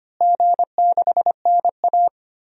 Morse code, heard as a keyed tone, about 25 wpm, 700 Hz. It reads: G6NA
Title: DXCC-100 Call Signs - 4 Characters - Repeated 2x - 25wpm